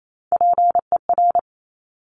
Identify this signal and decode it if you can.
Morse code, heard as a keyed tone, about 28 wpm, 700 Hz. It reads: PER
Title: Top Three Letter Combinations in Words 28wpm